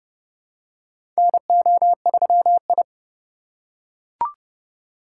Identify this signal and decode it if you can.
Morse code, heard as a keyed tone, about 29 wpm, 700 Hz. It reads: NO3I
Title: USA Call Signs - 2x Character Spacing - 30wpm